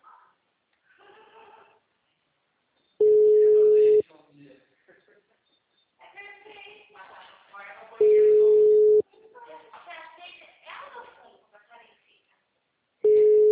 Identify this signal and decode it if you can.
Ringback tone (Europe, 425 Hz), with speech.
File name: out- -2003- - - .430